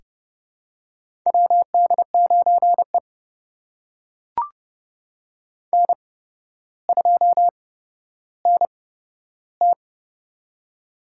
Morse code, heard as a keyed tone, about 29 wpm, spaced as for 13 wpm, 700 Hz. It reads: WD9E N2NT
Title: USA Call Signs - 4 Characters - 8x Character Spacing - 30wpm